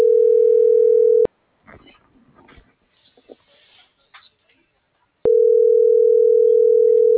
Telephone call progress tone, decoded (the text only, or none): ringback tone